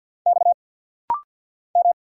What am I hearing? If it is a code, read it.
Morse code, heard as a keyed tone, about 50 wpm, 700 Hz. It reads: X M
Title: Single Letters - Rapid-Fire 50wpm